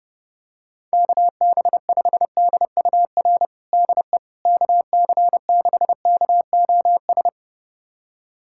Morse code, heard as a keyed tone, about 31 wpm, 700 Hz. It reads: KB5DUR DE KC6KOS
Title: USA Callsign DE Callsign - 6 Characters - 30wpm